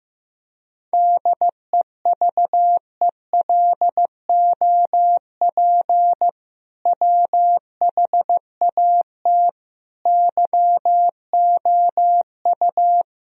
Morse code, heard as a keyed tone, about 15 wpm, 700 Hz. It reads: DEVELOP WHAT YOU
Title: Sentences from Top 1000 Words - Review Part 2 - 15wpm